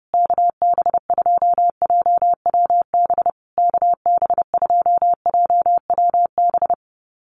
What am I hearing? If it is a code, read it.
Morse code, heard as a keyed tone, about 30 wpm, 700 Hz. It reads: KB2JWB KB2JWB
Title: USA Call Signs Repeated 3x 30wpm